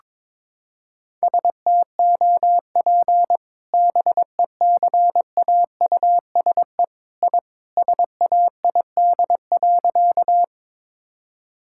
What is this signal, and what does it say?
Morse code, heard as a keyed tone, about 22 wpm, 700 Hz. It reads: STOP BECAUSE I SAID.
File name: practice / 22wpm / Sentences with 2-Word Phrases - 22wpm